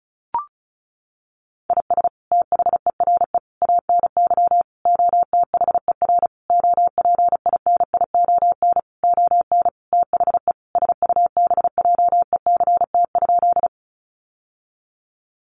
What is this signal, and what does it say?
Morse code, heard as a keyed tone, about 34 wpm, 700 Hz. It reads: IS THERE ANY OTHER OPINION ON THE SUBJECT?
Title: Sentences from Top 1200 Words 35wpm